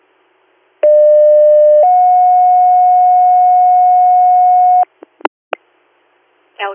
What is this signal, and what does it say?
Two-tone page, heard as a one-second tone, then a three-second tone, about 600 then 730 Hz, with speech. It is heard from 0.8 to 4.8 s.